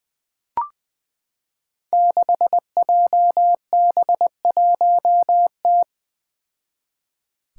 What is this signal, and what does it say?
Morse code, heard as a keyed tone, about 21 wpm, 700 Hz. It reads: 6JB1T
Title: Five Challenging Letter-Number Combinations 20wpm